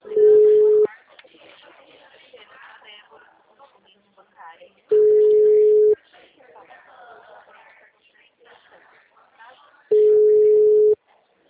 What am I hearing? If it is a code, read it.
Ringback tone (Europe, 425 Hz), with speech.
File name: out- -2005